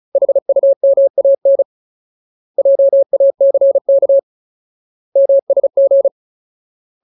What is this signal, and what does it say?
Morse code, heard as a keyed tone, about 34 wpm, 550 Hz. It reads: HUMAN JACK MSG
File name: Ultimate Word List / 35WPM Ultimate